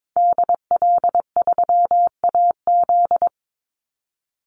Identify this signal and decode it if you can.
Morse code, heard as a keyed tone, about 22 wpm, 700 Hz. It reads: DL3AZ
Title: Call Signs - 22wpm